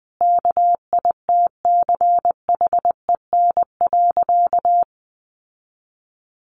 Morse code, heard as a keyed tone, about 20 wpm, 700 Hz. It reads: KITCHEN.
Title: Sentences from Top 1400 Words 20wpm